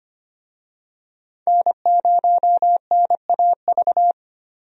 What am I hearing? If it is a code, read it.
Morse code, heard as a keyed tone, about 25 wpm, 700 Hz. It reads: N0NAV
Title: USA Call Signs - 5 Characters - Repeated 3x - 25wpm